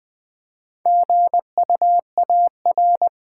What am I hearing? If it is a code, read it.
Morse code, heard as a keyed tone, about 20 wpm, 700 Hz. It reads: GUAR
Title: Build A Word - 1 Letter At A Time - 3 to 5-Letter Words - 20wpm